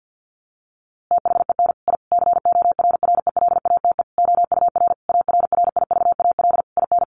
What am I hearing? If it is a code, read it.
Morse code, heard as a keyed tone, about 50 wpm, 700 Hz. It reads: THEN I CORRELATE OUR ARRIVAL IN
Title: A Sound Of Thunder 50wpm